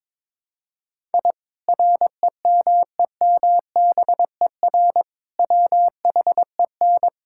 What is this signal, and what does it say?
Morse code, heard as a keyed tone, about 22 wpm, 700 Hz. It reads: I REMEMBER WHEN
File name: practice / 22wpm / Sentences with 2-Word Phrases - 22wpm